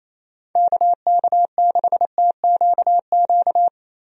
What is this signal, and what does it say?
Morse code, heard as a keyed tone, about 28 wpm, 700 Hz. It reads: KK6TQQ
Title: USA Call Signs - 6 Characters - Repeated 3x - 28wpm